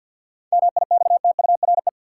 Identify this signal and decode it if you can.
Morse code, heard as a keyed tone, about 48 wpm, 700 Hz. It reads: MIXTURE
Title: Sentences from Top 1500 Words - New Word Drill - 50wpm